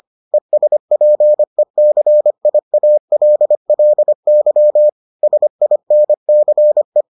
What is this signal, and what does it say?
Morse code, heard as a keyed tone, about 25 wpm, 600 Hz. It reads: ESPECIALLY SINCE